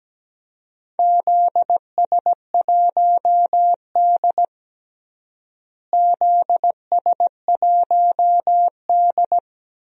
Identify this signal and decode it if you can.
Morse code, heard as a keyed tone, about 17 wpm, 700 Hz. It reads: ZS1D ZS1D
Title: DXCC-100 Call Signs - 4 Characters - Repeated 2x - 17wpm